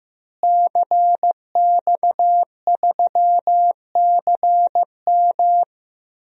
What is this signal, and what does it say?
Morse code, heard as a keyed tone, about 15 wpm, 700 Hz. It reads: CX3CM
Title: DXCC-100 Call Signs - 5 Characters - Repeated 3x - 15wpm